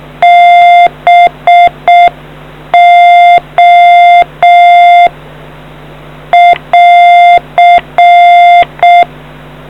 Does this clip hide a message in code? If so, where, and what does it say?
Yes: Morse code from 0.2 to 9.0 s, heard as a keyed tone, about 6 wpm, 700 Hz. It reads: BO <AR>